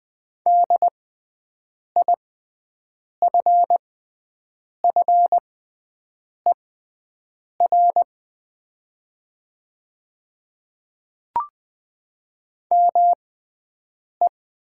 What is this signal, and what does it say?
Morse code, heard as a keyed tone, about 20 wpm, spaced as for 7 wpm, 700 Hz. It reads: DIFFER ME
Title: 6x Character Spacing - Top 300 Words - 20wpm